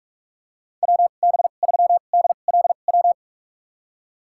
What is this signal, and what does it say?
Morse code, heard as a keyed tone, about 43 wpm, spaced as for 32 wpm, 690 Hz. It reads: WB3DLW
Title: USA Call Signs - 6 Characters - 2x Character Spacing - 45wpm